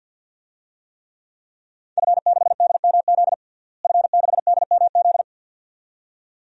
Morse code, heard as a keyed tone, about 52 wpm, 700 Hz. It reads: F6DMZ F6DMZ
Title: DXCC-100 Call Signs - 5 Characters - Repeated 2x - 50wpm